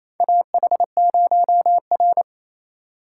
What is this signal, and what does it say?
Morse code, heard as a keyed tone, about 28 wpm, 700 Hz. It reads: AH0R
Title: USA Call Signs - 4 Characters - 2x Character Spacing - 28wpm